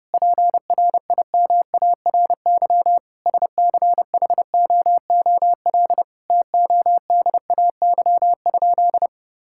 Morse code, heard as a keyed tone, about 30 wpm, 700 Hz. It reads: PRIMARY SCHOOL TODAY?
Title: Sentences from Top 1000 Words - Review Part 5 - 30wpm